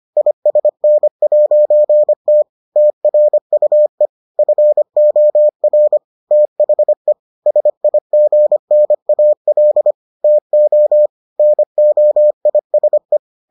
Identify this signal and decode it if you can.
Morse code, heard as a keyed tone, about 25 wpm, 600 Hz. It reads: ISN'T TRUE FOR THE SIGNAL TO NOISE